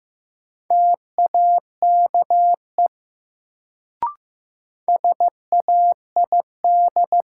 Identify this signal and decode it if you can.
Morse code, heard as a keyed tone, about 15 wpm, 700 Hz. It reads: TAKE SAID
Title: Sets of 2 Words - New Word Drill - 15wpm